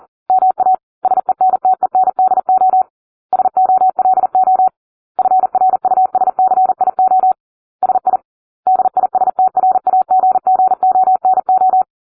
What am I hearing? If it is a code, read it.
Morse code, heard as a keyed tone, about 40 wpm, 750 Hz. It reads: MA SENTENDO SOLO FRUSCIO SI DISTRAGGONO